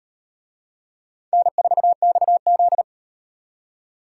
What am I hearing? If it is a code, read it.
Morse code, heard as a keyed tone, about 38 wpm, 700 Hz. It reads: N4XZ